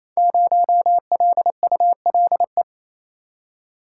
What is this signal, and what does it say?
Morse code, heard as a keyed tone, about 29 wpm, 700 Hz. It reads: 0LULE
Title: Code Groups - 5 Letter:Numbers - 28wpm